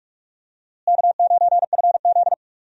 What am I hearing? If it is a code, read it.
Morse code, heard as a keyed tone, about 41 wpm, 700 Hz. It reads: K9FZ
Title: USA Call Signs - 4 Characters - Repeated 2x - 45wpm